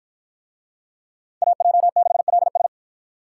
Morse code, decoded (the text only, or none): AJ6LS